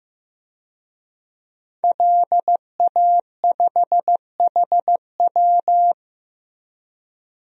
Morse code, heard as a keyed tone, about 15 wpm, 700 Hz. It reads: LA5HW
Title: DXCC-100 Call Signs - 5 Characters - Repeated 3x - 15wpm